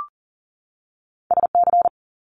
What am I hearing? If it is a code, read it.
Morse code, heard as a keyed tone, about 39 wpm, 700 Hz. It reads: SC
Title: Multi-letter Phonemes 40wpm